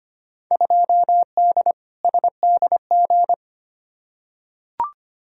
Morse code, heard as a keyed tone, about 25 wpm, 700 Hz. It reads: 2D SDG
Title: Field Day Exchanges Encore - 25wpm